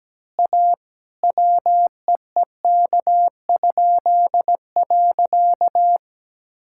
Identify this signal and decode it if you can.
Morse code, heard as a keyed tone, about 17 wpm, 700 Hz. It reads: A WEEK?.
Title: Pangram Challenge v2 - 17wpm